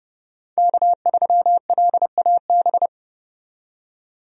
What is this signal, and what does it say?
Morse code, heard as a keyed tone, about 30 wpm, 700 Hz. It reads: K3LAB
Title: USA Call Signs - 5 Characters - 30wpm